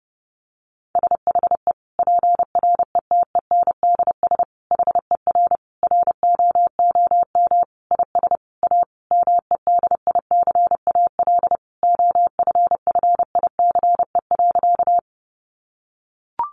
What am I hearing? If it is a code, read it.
Morse code, heard as a keyed tone, about 30 wpm, 700 Hz. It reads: SHE PRETENDS HER ROOM IS A MEDICAL OFFICE.
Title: Additional Easy Phrases 30wpm